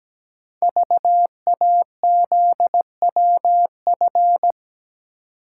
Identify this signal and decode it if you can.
Morse code, heard as a keyed tone, about 17 wpm, 700 Hz. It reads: VAZWF
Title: Code Groups - 5 Letter:Numbers - 17wpm